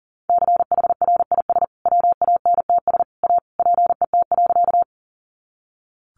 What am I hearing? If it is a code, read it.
Morse code, heard as a keyed tone, about 40 wpm, 700 Hz. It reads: CHRIS WANTS A PET.
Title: Additional Easy Phrases 40wpm